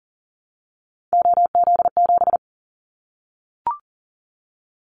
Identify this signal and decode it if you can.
Morse code, heard as a keyed tone, about 42 wpm, 700 Hz. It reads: OZ7
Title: Callsign Prefixes for Top 50 DXCC - 40wpm